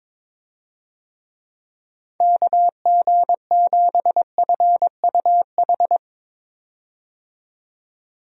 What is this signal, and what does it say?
Morse code, heard as a keyed tone, about 22 wpm, 700 Hz. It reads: KG7FUH
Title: USA Call Signs - 6 Characters - 22wpm